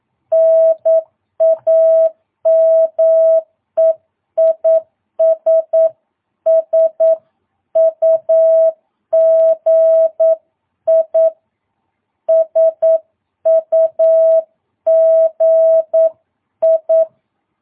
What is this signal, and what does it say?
Morse code, heard as a keyed tone, about 9 wpm, 650 Hz. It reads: NAME IS SUGI SUGI
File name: jh8cbh / 2022cw